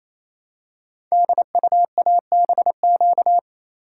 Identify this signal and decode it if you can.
Morse code, heard as a keyed tone, about 27 wpm, 700 Hz. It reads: DUABQ
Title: Code Groups - 5 Letters - 28wpm